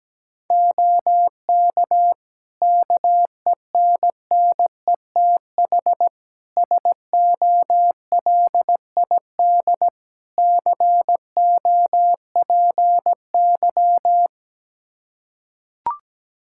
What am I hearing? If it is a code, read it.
Morse code, heard as a keyed tone, about 17 wpm, 700 Hz. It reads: OK KENNETH SOLID COPY
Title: Practice QSO - Exchange 3 - More Info - 17wpm